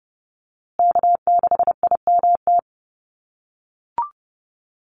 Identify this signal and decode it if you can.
Morse code, heard as a keyed tone, about 30 wpm, 700 Hz. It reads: K6IMT
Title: USA Call Signs Repeated 3x 30wpm